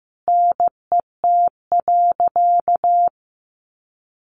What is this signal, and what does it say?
Morse code, heard as a keyed tone, about 14 wpm, 700 Hz. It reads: NET.
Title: Sentences from Top 1500 Words 15wpm